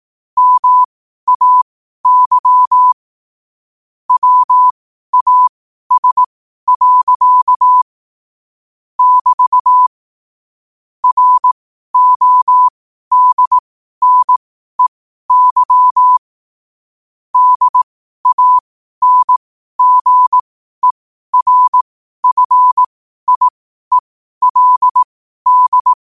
Morse code, heard as a keyed tone, about 18 wpm, spaced as for 12 wpm, 1000 Hz. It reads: MAY WAS. <BT> RODNEY DANGERFIELD
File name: qotdfiles / 12wpm